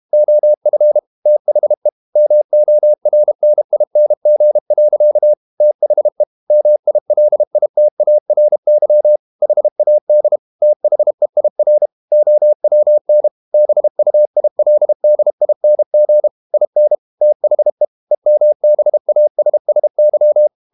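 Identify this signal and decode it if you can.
Morse code, heard as a keyed tone, about 32 wpm, 600 Hz. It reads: OF THE MORNING. THE MILITARY HAD THEIR OWN BUILDING IN THE EMBASSY
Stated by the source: Morse code practice at 32 wpm from CW med Gnister, reading Ken Follett's novel Never in English.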